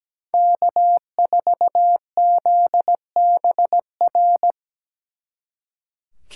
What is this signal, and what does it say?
Morse code, heard as a keyed tone, about 20 wpm, 700 Hz. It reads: K4ZBR
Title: Speed Racing - USA 1x3 Call Signs 17wpm